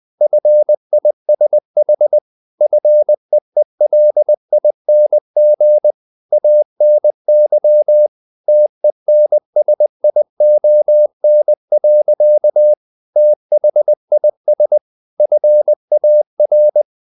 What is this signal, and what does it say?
Morse code, heard as a keyed tone, about 20 wpm, 600 Hz. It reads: FISH FEELING ANY TENSION. THIS FAR
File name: the old man and the sea / the old man and the sea part 1 20wpm